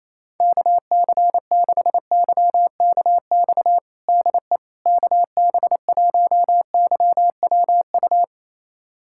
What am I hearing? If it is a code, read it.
Morse code, heard as a keyed tone, about 28 wpm, 700 Hz. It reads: KC6YKX DE KB1YWU
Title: USA Callsign DE Callsign - 6 Characters - 28wpm